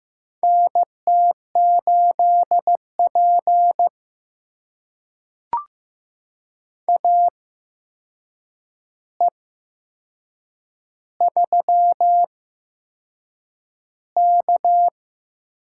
Morse code, heard as a keyed tone, about 15 wpm, spaced as for 6 wpm, 700 Hz. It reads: NT8P AE3K
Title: USA Call Signs - 4 Characters - 8x Character Spacing - 15wpm